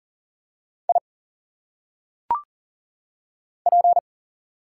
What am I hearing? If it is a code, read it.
Morse code, heard as a keyed tone, about 40 wpm, 700 Hz. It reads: I P
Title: Single Letters - Phonetic Pronunciation 40wpm